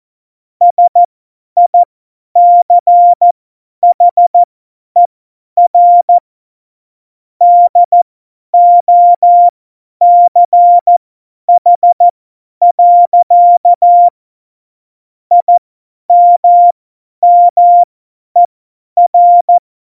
Morse code, heard as a keyed tone, about 14 wpm, spaced as for 10 wpm, 700 Hz. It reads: SICHER DOCH. IMMER
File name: Morsekurs / Witze / Witz 20. Februar 2018, 50 F70